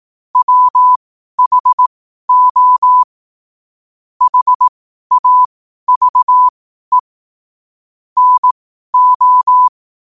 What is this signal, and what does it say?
Morse code, heard as a keyed tone, about 18 wpm, spaced as for 12 wpm, 1000 Hz. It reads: WHO HAVE NO
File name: qotdfiles / 12wpm